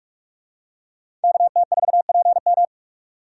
Morse code, heard as a keyed tone, about 44 wpm, 700 Hz. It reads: KT4PM
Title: USA Call Signs - 5 Characters - 6x Character Spacing - 45wpm